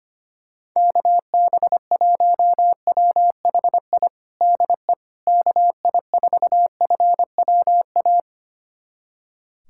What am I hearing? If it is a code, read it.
Morse code, heard as a keyed tone, about 25 wpm, 700 Hz. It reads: KB1WHI DE KI4FWA
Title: USA Callsign DE Callsign - 6 Characters - 25wpm